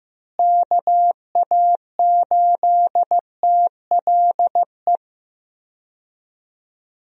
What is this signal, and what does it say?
Morse code, heard as a keyed tone, about 16 wpm, 700 Hz. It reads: KA8TLE
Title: USA Call Signs - 6 Characters - Repeated 2x - 15wpm